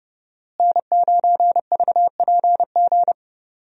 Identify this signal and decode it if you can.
Morse code, heard as a keyed tone, about 29 wpm, 700 Hz. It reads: N9VPG
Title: USA Call Signs - 5 Characters - Repeated 2x - 30wpm